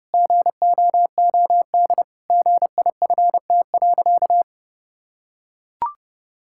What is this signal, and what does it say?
Morse code, heard as a keyed tone, about 30 wpm, 700 Hz. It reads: GOOD GIFT.
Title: Sentences from Top 1000 Words - Review Part 6 - 30wpm